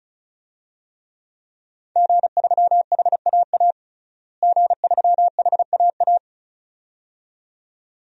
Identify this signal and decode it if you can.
Morse code, heard as a keyed tone, about 34 wpm, 700 Hz. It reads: G3HAA G3HAA
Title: DXCC-100 Call Signs - 5 Characters - Repeated 2x - 35wpm